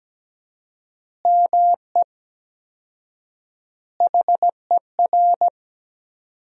Morse code, heard as a keyed tone, about 17 wpm, 700 Hz. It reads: ME HER
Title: 4x Word-Spacing - Sentences from Top 500 Words - 17wpm